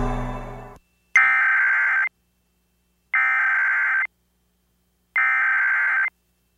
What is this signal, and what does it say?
Emergency Alert System (SAME) tones (header bursts), heard as harsh buzzing bursts. It is heard from 1.2 to 6.1 s.